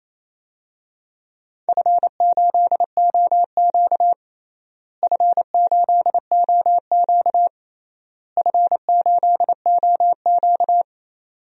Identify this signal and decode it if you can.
Morse code, heard as a keyed tone, about 28 wpm, 700 Hz. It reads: F8OQ F8OQ F8OQ
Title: DXCC-100 Call Signs - 4 Characters - Repeated 3x - 28wpm